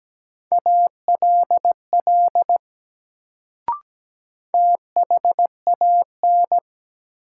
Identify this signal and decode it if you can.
Morse code, heard as a keyed tone, about 17 wpm, 700 Hz. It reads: ALL THAN
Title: Sentences from Top 100 Words - New Word Drill - 17wpm